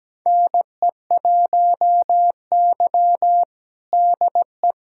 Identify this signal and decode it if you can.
Morse code, heard as a keyed tone, about 17 wpm, 700 Hz. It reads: NE1Y DE
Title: USA Callsign DE Callsign 17wpm